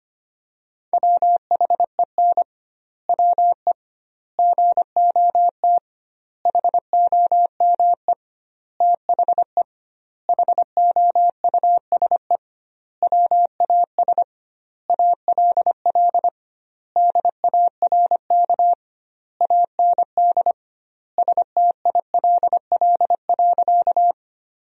Morse code, heard as a keyed tone, about 25 wpm, 700 Hz. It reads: WHEN WE GOT HOME THE HOUSE WAS ALL DARK AND STILL.
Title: 2x Word-Spacing - Sentences from Top 500 Words - 25wpm